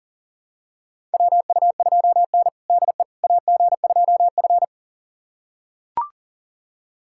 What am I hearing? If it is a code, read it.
Morse code, heard as a keyed tone, about 40 wpm, 700 Hz. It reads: WU2N DE AG2F
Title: USA Callsign DE Callsign - 4 Characters - 40wpm